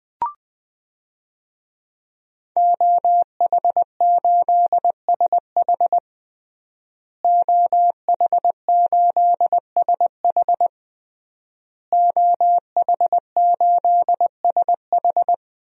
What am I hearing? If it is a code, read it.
Morse code, heard as a keyed tone, about 20 wpm, 700 Hz. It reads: OH8SH OH8SH OH8SH
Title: DXCC-100 Call Signs - 5 Characters - Repeated 3x - 20wpm